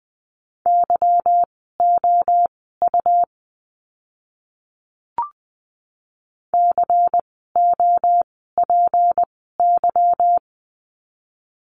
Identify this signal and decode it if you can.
Morse code, heard as a keyed tone, about 20 wpm, spaced as for 14 wpm, 700 Hz. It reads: YOU COPY
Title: 153 Lesson 25f - Letter p - Words 20wpm